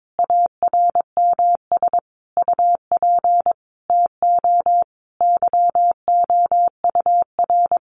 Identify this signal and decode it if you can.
Morse code, heard as a keyed tone, about 22 wpm, 700 Hz. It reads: ARMS UP TO YOUR
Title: A Sound Of Thunder 22wpm